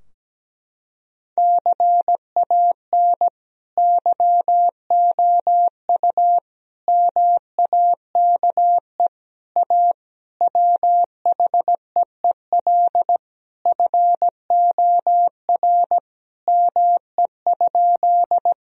Morse code, heard as a keyed tone, about 17 wpm, 700 Hz. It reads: CAN YOU MAKE A WHEEL FOR ME?
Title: Sentences from Top 1000 Words - Review Part 3 - 17wpm